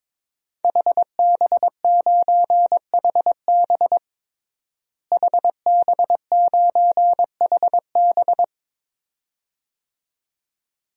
Morse code, heard as a keyed tone, about 22 wpm, 700 Hz. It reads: HB9HB HB9HB
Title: DXCC-100 Call Signs - 5 Characters - Repeated 2x - 22wpm